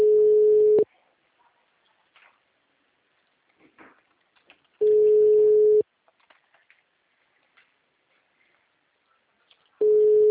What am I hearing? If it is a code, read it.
Ringback tone (Europe, 425 Hz).